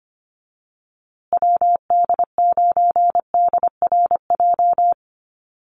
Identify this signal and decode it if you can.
Morse code, heard as a keyed tone, about 25 wpm, 700 Hz. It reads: WD9DRJ
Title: USA Call Signs Repeated 3x 25wpm